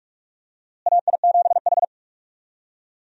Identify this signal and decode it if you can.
Morse code, heard as a keyed tone, about 42 wpm, 700 Hz. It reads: AI7H
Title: USA Call Signs - 4 Characters - Repeated 2x - 45wpm